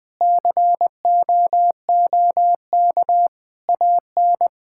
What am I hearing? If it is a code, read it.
Morse code, heard as a keyed tone, about 19 wpm, 700 Hz. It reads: COOK AN
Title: Sentences with 7-Word Phrases - 20wpm